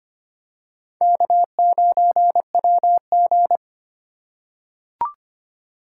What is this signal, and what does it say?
Morse code, heard as a keyed tone, about 25 wpm, 700 Hz. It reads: K9WG
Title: USA Call Signs - 2x Character Spacing - 25wpm